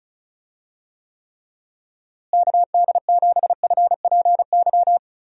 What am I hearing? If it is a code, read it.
Morse code, heard as a keyed tone, about 34 wpm, 700 Hz. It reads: KD7FPY
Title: USA Call Signs - 6 Characters - Repeated 3x - 35wpm